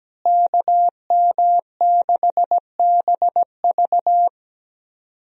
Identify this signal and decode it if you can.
Morse code, heard as a keyed tone, about 17 wpm, 700 Hz. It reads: KM6BV
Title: USA Call Signs - 5 Characters - Repeated 2x - 17wpm